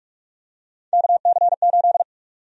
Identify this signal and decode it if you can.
Morse code, heard as a keyed tone, about 42 wpm, 700 Hz. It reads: KC8
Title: USA Call Sign Prefixes - Repeated 2x - 45wpm